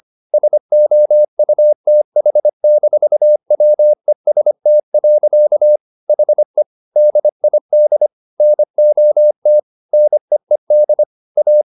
Morse code, heard as a keyed tone, about 25 wpm, 600 Hz. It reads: SOUTH-WEST. HE DID NOT NEED A